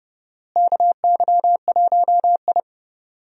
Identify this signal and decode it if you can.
Morse code, heard as a keyed tone, about 32 wpm, 700 Hz. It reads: KY1I
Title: USA Call Signs - 4 Characters - Repeated 2x - 30wpm